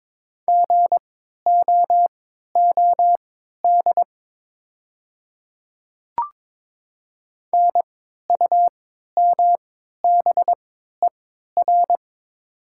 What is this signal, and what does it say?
Morse code, heard as a keyed tone, about 22 wpm, spaced as for 12 wpm, 700 Hz. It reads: GOOD NUMBER
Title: 3x Character Spacing - Top 200 Words - 22wpm